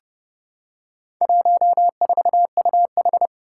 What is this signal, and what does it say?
Morse code, heard as a keyed tone, about 32 wpm, 700 Hz. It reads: 14UH
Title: Four Challenging Letter-Number Combinations 30wpm